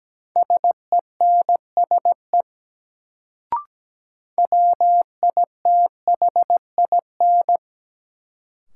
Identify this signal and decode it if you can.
Morse code, heard as a keyed tone, about 17 wpm, 700 Hz. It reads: SENSE WITHIN
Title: Sentences from Top 700 Words - New Word Drill - 17wpm